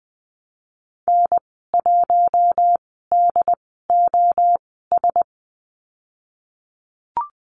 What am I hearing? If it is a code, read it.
Morse code, heard as a keyed tone, about 20 wpm, spaced as for 14 wpm, 700 Hz. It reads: N1DOS
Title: 069 Lesson 12g - Letter d - Call Signs 20wpm